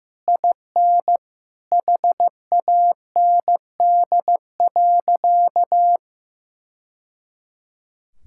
Morse code, heard as a keyed tone, about 15 wpm, 700 Hz. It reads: IN HAND.
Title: Sentences from Top 1000 Words - Review Part 5 - 15wpm